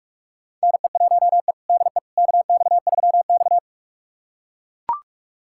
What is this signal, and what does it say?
Morse code, heard as a keyed tone, about 43 wpm, 700 Hz. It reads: NE1E DE KX3X
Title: USA Callsign DE Callsign - 4 Characters - 45wpm